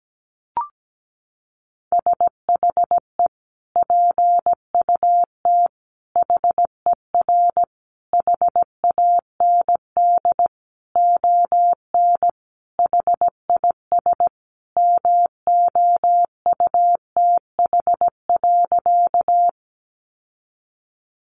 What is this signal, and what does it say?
Morse code, heard as a keyed tone, about 17 wpm, 700 Hz. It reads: SHE PUT HER HAND ON HIS MOUTH.
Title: Sentences from Top 1500 Words 17wpm